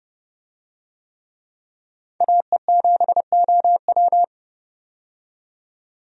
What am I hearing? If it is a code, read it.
Morse code, heard as a keyed tone, about 30 wpm, 700 Hz. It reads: AE7OW
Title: USA Call Signs - 5 Characters - 30wpm